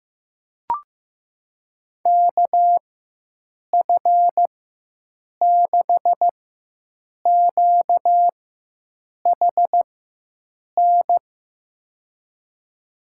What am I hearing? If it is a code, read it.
Morse code, heard as a keyed tone, about 15 wpm, spaced as for 7 wpm, 700 Hz. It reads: KF6QHN
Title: USA Call Signs - 4x Character Spacing - 15wpm